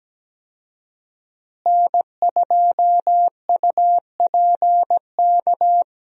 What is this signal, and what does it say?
Morse code, heard as a keyed tone, about 17 wpm, 700 Hz. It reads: N2UPK
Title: USA Call Signs - 5 Characters - Repeated 3x - 17wpm